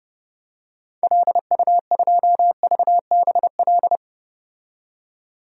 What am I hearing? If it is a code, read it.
Morse code, heard as a keyed tone, about 30 wpm, 700 Hz. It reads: LU2VBL
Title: DXCC-100 Call Signs - 6 Characters - 30wpm